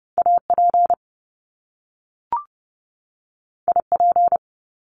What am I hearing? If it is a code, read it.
Morse code, heard as a keyed tone, about 30 wpm, 700 Hz. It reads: AP IP
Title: Top Two Letter Combinations in Words 30wpm